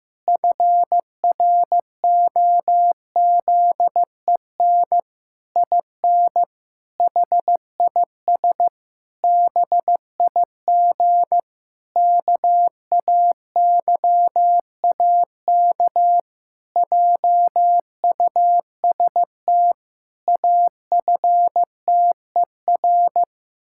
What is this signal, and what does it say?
Morse code, heard as a keyed tone, about 15 wpm, 700 Hz. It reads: FROZEN IN HIS BIG KAYAK JUST AFTER
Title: Pangram Challenge v2 - 15wpm